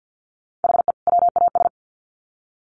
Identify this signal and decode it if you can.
Morse code, heard as a keyed tone, about 50 wpm, 700 Hz. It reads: HE WAS
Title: Sets of 2 Words 50wpm